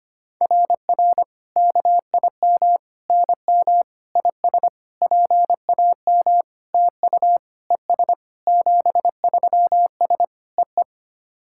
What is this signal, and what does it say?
Morse code, heard as a keyed tone, about 25 wpm, 700 Hz. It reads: RR KIM NM IS PAM TU ES 73S EE
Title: SOTA QSO Practice 25wpm